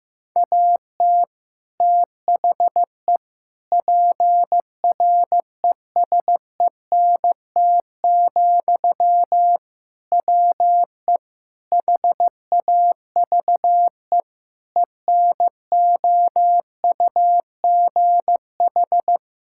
Morse code, heard as a keyed tone, about 15 wpm, 700 Hz. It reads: AT THE PRESENT, WE HAVE ENOUGH
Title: Sentences from Top 1000 Words - Review Part 5 - 15wpm